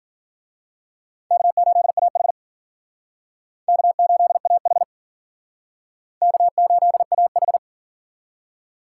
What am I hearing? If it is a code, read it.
Morse code, heard as a keyed tone, about 47 wpm, 700 Hz. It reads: K8AH K8AH K8AH
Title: Speed Racing - USA 1x2 Call Signs 40wpm